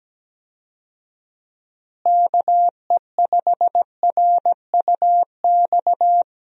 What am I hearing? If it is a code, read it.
Morse code, heard as a keyed tone, about 17 wpm, 700 Hz. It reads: KE5RUX
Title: USA Call Signs - 6 Characters - Repeated 2x - 17wpm